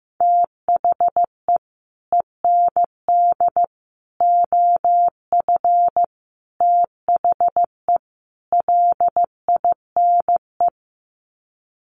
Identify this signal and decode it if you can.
Morse code, heard as a keyed tone, about 14 wpm, 700 Hz. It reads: THE END OF THE LINE
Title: Sets of 5 Words Encore2 15wpm